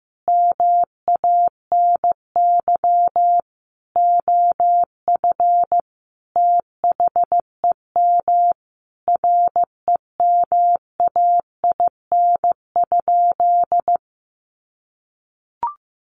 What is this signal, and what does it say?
Morse code, heard as a keyed tone, about 14 wpm, 700 Hz. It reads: MANY OF THEM REMAIN?
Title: Sentences from Top 1200 Words 15wpm